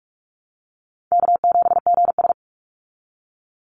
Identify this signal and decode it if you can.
Morse code, heard as a keyed tone, about 44 wpm, 700 Hz. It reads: K7GS
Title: Easy Call Signs Repeated 2x - 45wpm